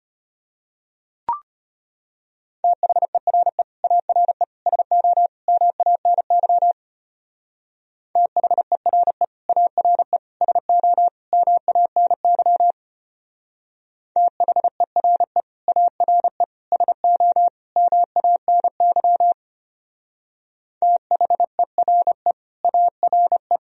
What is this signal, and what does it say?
Morse code, heard as a keyed tone, about 31 wpm, 700 Hz. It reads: THERE ARE SO MANY THERE ARE SO MANY THERE ARE SO MANY THERE ARE
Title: Speed-Racing - Sets of 4 Words 25wpm